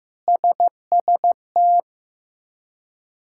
Morse code, heard as a keyed tone, about 15 wpm, 700 Hz. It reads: SST
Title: K1USN SST QSO Practice - 15wpm